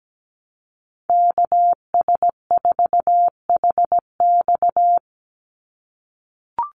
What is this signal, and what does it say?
Morse code, heard as a keyed tone, about 17 wpm, 700 Hz. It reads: KS4HX
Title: USA Call Signs 17wpm